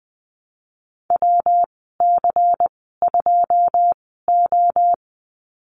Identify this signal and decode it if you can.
Morse code, heard as a keyed tone, about 19 wpm, spaced as for 14 wpm, 700 Hz. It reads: WC2O
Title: 161 Lesson 26g - Letter g - Call Signs 20wpm